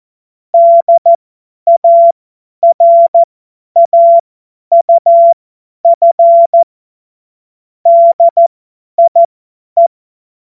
Morse code, heard as a keyed tone, about 14 wpm, spaced as for 10 wpm, 680 Hz. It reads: DARAUF DIE
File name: Morsekurs / Witze / Witz 06. April 2021, 50 F70